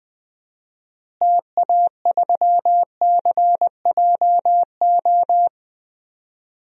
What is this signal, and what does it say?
Morse code, heard as a keyed tone, about 20 wpm, 700 Hz. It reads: TA3CJO
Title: DXCC-100 Call Signs - 6 Characters - Repeated 3x - 20wpm